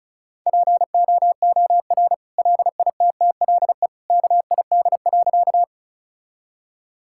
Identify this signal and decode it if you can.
Morse code, heard as a keyed tone, about 34 wpm, 700 Hz. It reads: POOR LITTLE KID.
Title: Sentences from Top 1600 Words 35wpm